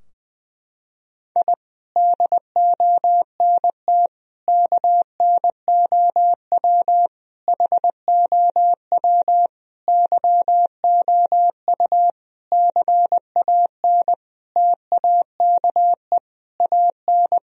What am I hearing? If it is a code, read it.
Morse code, heard as a keyed tone, about 21 wpm, 700 Hz. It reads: I DONT KNOW HOW YOU CAN TAKE AN
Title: Sentences from Top 1000 Words - Review Part 2 - 20wpm